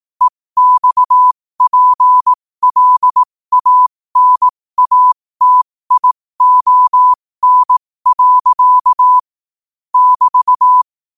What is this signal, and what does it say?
Morse code, heard as a keyed tone, about 18 wpm, spaced as for 15 wpm, 1000 Hz. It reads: EXPLANATION. <BT>
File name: qotdfiles / 15wpm